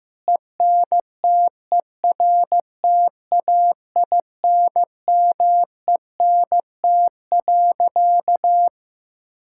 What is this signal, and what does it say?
Morse code, heard as a keyed tone, about 15 wpm, 700 Hz. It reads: ENTERTAINMENT.
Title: Sentences from Top 1000 Words - Review Part 6 - 15wpm